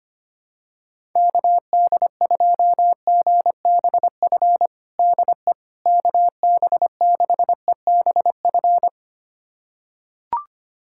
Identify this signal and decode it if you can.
Morse code, heard as a keyed tone, about 25 wpm, 700 Hz. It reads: KD2GBF DE KB6EBF
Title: USA Callsign DE Callsign 25wpm